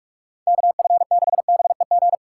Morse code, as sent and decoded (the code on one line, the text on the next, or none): -.- ..-. -.... -... --.
KF6BEG